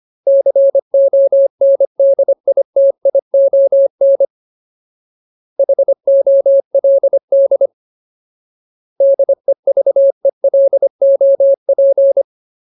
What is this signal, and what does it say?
Morse code, heard as a keyed tone, about 25 wpm, 550 Hz. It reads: CONDITION HOLD DEVELOP
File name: Ultimate Word List / 25WPM Ultimate